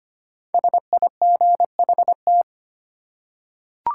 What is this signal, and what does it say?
Morse code, heard as a keyed tone, about 25 wpm, 700 Hz. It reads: SIGHT
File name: 5 Letter Words - First 3 Letters Given - 25wpm